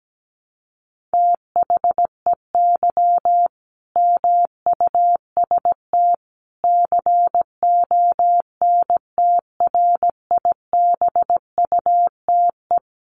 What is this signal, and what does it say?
Morse code, heard as a keyed tone, about 17 wpm, 700 Hz. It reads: THEY MUST CONTRIBUTE
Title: Sentences from Top 1400 Words 17wpm